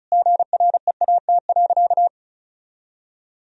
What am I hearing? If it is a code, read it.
Morse code, heard as a keyed tone, about 36 wpm, 700 Hz. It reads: GREAT.
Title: Sentences from Top 1000 Words - Review Part 4 - 35wpm